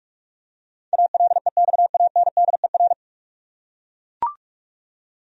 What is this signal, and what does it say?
Morse code, heard as a keyed tone, about 44 wpm, 700 Hz. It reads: ALEXANDER
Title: USA Common Names - 45wpm